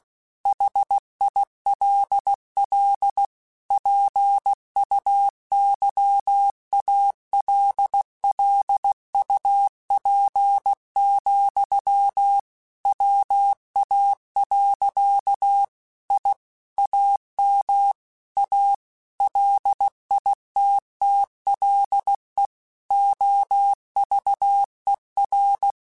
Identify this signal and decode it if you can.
Morse code, heard as a keyed tone, about 16 wpm, 800 Hz. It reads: HILL PUYALLUP, WA. I AM A LITTLE OVER
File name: QSOs / QSOs-mp3s / WA7QOX 2@15